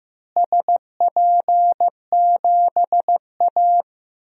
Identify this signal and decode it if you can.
Morse code, heard as a keyed tone, about 16 wpm, 700 Hz. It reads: SP7A
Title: DXCC-100 Call Signs - 4 Characters - Repeated 3x - 15wpm